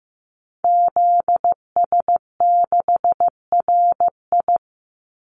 Morse code, heard as a keyed tone, about 15 wpm, 700 Hz. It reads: ZS6RI
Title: Call Signs Repeated 3x - 15wpm